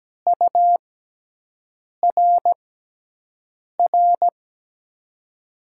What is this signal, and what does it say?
Morse code, heard as a keyed tone, about 17 wpm, spaced as for 6 wpm, 700 Hz. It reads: URR
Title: USA Call Signs - 5 Characters - 6x Character Spacing - 17wpm